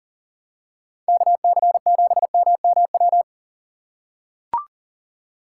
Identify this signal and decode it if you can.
Morse code, heard as a keyed tone, about 40 wpm, 700 Hz. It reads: KC7MMW
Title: USA Call Signs - 6 Characters - 3x Character Spacing - 40wpm